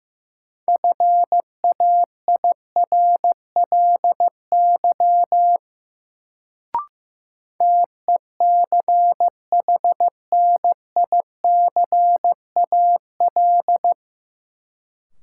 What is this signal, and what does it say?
Morse code, heard as a keyed tone, about 15 wpm, 700 Hz. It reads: FAIRLY TECHNICAL
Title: Sentences from Top 1100 Words - New Word Drill - 15wpm